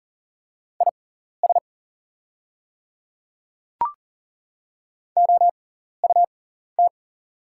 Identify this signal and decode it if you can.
Morse code, heard as a keyed tone, about 40 wpm, spaced as for 14 wpm, 690 Hz. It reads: IS OUT
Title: 6x Character Spacing - Top 100 Words - 40wpm